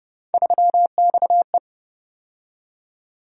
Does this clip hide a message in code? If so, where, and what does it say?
Yes: Morse code from 0.3 to 1.6 s, heard as a keyed tone, about 29 wpm, 700 Hz. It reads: 3XE